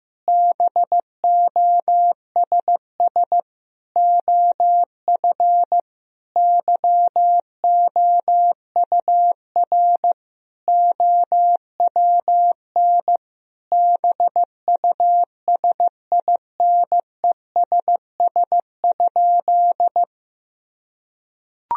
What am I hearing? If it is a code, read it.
Morse code, heard as a keyed tone, about 15 wpm, 700 Hz. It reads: BOSS OF YOUR OWN BUSINESS?
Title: Sentences from Top 1000 Words - Review Part 4 - 15wpm